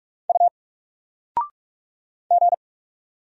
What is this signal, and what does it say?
Morse code, heard as a keyed tone, about 44 wpm, 700 Hz. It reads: U G
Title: Single Letter-Number 45wpm